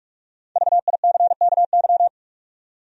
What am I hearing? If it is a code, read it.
Morse code, heard as a keyed tone, about 46 wpm, 700 Hz. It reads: VICKY
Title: USA Common Names - Max 5 Letters - 45wpm